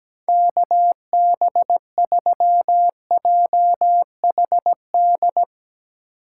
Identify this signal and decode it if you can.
Morse code, heard as a keyed tone, about 17 wpm, 700 Hz. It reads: KB3JHD
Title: USA Call Signs - 6 Characters - 6x Character Spacing - 17wpm